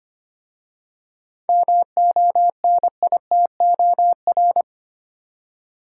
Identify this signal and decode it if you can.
Morse code, heard as a keyed tone, about 25 wpm, 700 Hz. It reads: MONITOR
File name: practice / 25wpm / Easy Words with Hints - 25wpm